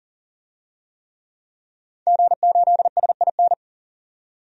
Morse code, heard as a keyed tone, about 40 wpm, 700 Hz. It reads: G8SIN
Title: DXCC-100 Call Signs - 5 Characters - 40wpm